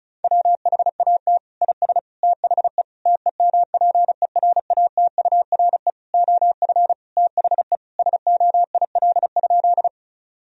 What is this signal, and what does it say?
Morse code, heard as a keyed tone, about 34 wpm, 700 Hz. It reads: WHAT IS THE TEMPERATURE OF THE SOIL?
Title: Sentences from Top 1000 Words - Review Part 4 - 35wpm